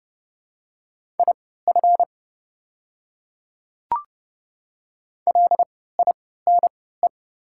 Morse code, heard as a keyed tone, about 30 wpm, spaced as for 17 wpm, 700 Hz. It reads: IF LINE
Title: 3x Character Spacing - Top 300 Words - 30wpm